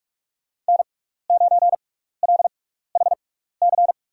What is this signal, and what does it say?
Morse code, heard as a keyed tone, about 46 wpm, spaced as for 16 wpm, 700 Hz. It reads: N9LHC
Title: USA Call Signs - 5 Characters - 6x Character Spacing - 45wpm